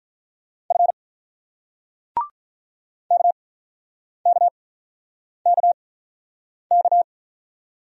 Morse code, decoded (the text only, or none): F KKKK